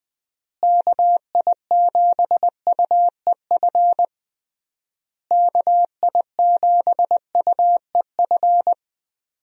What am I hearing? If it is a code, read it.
Morse code, heard as a keyed tone, about 20 wpm, 700 Hz. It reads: KI7UEF KI7UEF
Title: USA Call Signs - 6 Characters - Repeated 3x - 20wpm